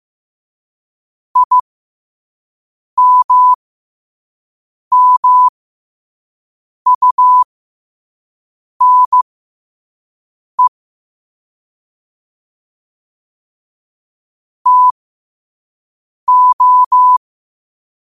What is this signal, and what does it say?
Morse code, heard as a keyed tone, about 15 wpm, spaced as for 5 wpm, 1000 Hz. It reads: IMMUNE TO
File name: qotdfiles / 5wpm